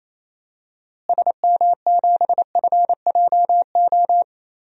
Morse code, heard as a keyed tone, about 27 wpm, 700 Hz. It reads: SM7FJO
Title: DXCC-100 Call Signs - 6 Characters - Repeated 2x - 28wpm